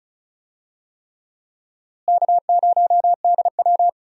Morse code, heard as a keyed tone, about 36 wpm, 700 Hz. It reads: K0DW
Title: USA Call Signs - 4 Characters - Repeated 3x - 35wpm